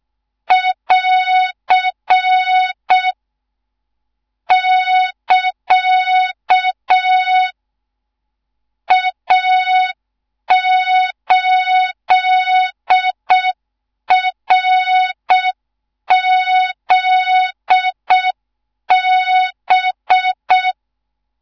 Morse code, heard as a keyed tone, about 6 wpm, 3700 Hz. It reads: <AR> <KA> A8RZB
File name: Morze-HA2MN / hang / 14 280 HA2MN